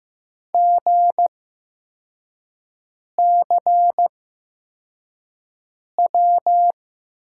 Morse code, heard as a keyed tone, about 15 wpm, spaced as for 4 wpm, 700 Hz. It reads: GCW
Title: USA Call Signs - 6 Characters - 8x Character Spacing - 15wpm